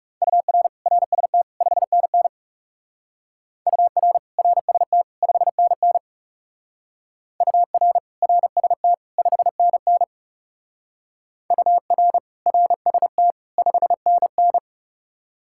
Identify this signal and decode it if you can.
Morse code, heard as a keyed tone, about 37 wpm, 700 Hz. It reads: UR RST 5NN UR RST 5NN UR RST 5NN UR RST 5NN
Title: Speed-Racing Common QSO Phrases Encore - 30wpm